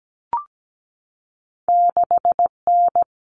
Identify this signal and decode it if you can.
Morse code, heard as a keyed tone, about 17 wpm, 700 Hz. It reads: 6N
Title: Common Cut Numbers - 2-Digit 17wpm